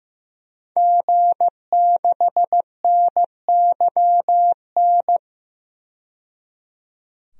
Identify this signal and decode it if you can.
Morse code, heard as a keyed tone, about 15 wpm, 700 Hz. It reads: G6NYN
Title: DXCC-100 Call Signs - 5 Characters - 15wpm